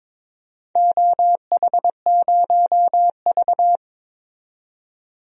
Morse code, heard as a keyed tone, about 22 wpm, 700 Hz. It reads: OH0V
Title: DXCC-100 Call Signs - 4 Characters - 22wpm